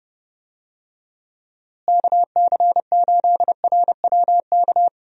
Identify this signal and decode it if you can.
Morse code, heard as a keyed tone, about 30 wpm, 700 Hz. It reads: KC8RWK
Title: USA Call Signs - 6 Characters - Repeated 2x - 30wpm